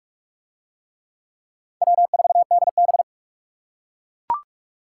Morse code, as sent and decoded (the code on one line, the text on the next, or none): .-- ....- -.. -...
W4DB